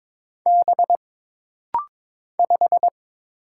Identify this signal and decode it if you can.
Morse code, heard as a keyed tone, about 22 wpm, 700 Hz. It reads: B 5
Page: This rapid-fire timing is geared to push your ICR (Instant Character Recognition) skills.